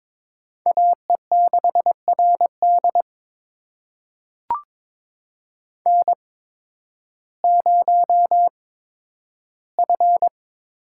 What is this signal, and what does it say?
Morse code, heard as a keyed tone, about 21 wpm, spaced as for 12 wpm, 700 Hz. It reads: AE6RD N0F
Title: USA Call Signs - 5 Characters - 8x Character Spacing - 22wpm